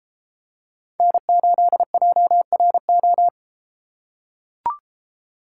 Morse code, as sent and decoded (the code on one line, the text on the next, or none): -. ---.. .--- .-. ---
N8JRO